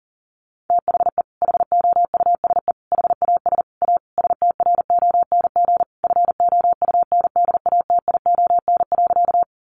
Morse code, heard as a keyed tone, about 40 wpm, 700 Hz. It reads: THE HOUSE HAS A STRONG FOUNDATION.
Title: Sentences from Top 1200 Words 40wpm